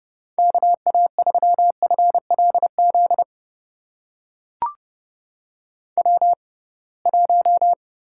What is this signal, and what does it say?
Morse code, heard as a keyed tone, about 29 wpm, spaced as for 22 wpm, 700 Hz. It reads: KA3FLZ W1
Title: USA Call Signs - 6x Character Spacing - 30wpm